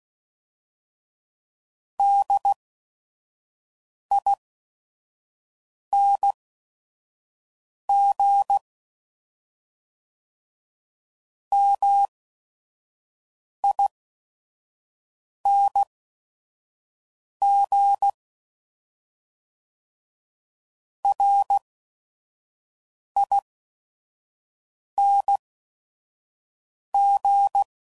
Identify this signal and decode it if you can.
Morse code, heard as a keyed tone, about 16 wpm, spaced as for 5 wpm, 800 Hz. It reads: DING MING RING
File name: Suffixes / Special ING's / VerySlowToFaster ING's